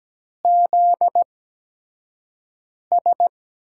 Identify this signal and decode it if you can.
Morse code, heard as a keyed tone, about 17 wpm, spaced as for 5 wpm, 700 Hz. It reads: ZS
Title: USA Call Signs - 8x Character Spacing - 17wpm